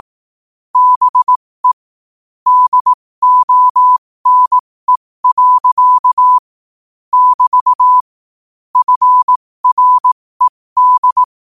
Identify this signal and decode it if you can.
Morse code, heard as a keyed tone, about 18 wpm, spaced as for 15 wpm, 1000 Hz. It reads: BE DONE. <BT> FRED
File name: qotdfiles / 15wpm